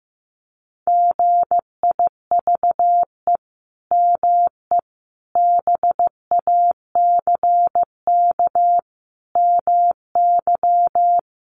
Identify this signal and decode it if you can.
Morse code, heard as a keyed tone, about 15 wpm, 700 Hz. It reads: GIVE ME BACK MY
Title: Sentences from Top 800 Words 15wpm